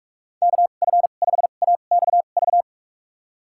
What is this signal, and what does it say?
Morse code, heard as a keyed tone, about 42 wpm, spaced as for 31 wpm, 700 Hz. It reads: KF5AXV
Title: USA Call Signs - 2x Character Spacing - 45wpm